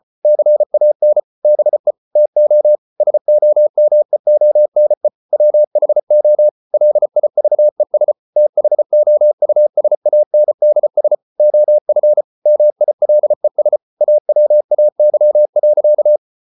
Morse code, heard as a keyed tone, about 34 wpm, 600 Hz. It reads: CAN BE TO SOMEONE WHO LIVES THOUSANDS OF MILES AWAY.